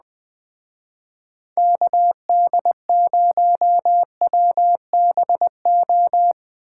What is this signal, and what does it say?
Morse code, heard as a keyed tone, about 20 wpm, 700 Hz. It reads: KD0WBO
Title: Speed Racing - USA Call Signs 17wpm